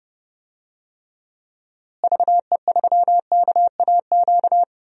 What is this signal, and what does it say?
Morse code, heard as a keyed tone, about 30 wpm, 700 Hz. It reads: VE3KAQ
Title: DXCC-100 Call Signs - 6 Characters - Repeated 2x - 30wpm